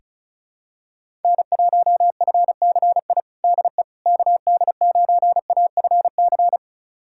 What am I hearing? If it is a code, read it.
Morse code, heard as a keyed tone, about 34 wpm, 700 Hz. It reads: N1FCI DE KD9AFC
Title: USA Callsign DE Callsign 35wpm